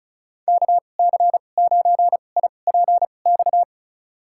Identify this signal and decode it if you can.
Morse code, heard as a keyed tone, about 36 wpm, spaced as for 25 wpm, 700 Hz. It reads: KC9IPX
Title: USA Call Signs - 6 Characters - 2x Character Spacing - 35wpm